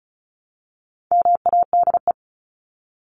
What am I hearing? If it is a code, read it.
Morse code, heard as a keyed tone, about 34 wpm, 700 Hz. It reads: MADE
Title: Top 200 Words 35wpm